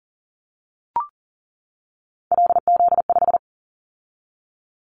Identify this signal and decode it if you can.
Morse code, heard as a keyed tone, about 42 wpm, 700 Hz. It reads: LZ5
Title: Callsign Prefixes for Top 25 DXCC - 40wpm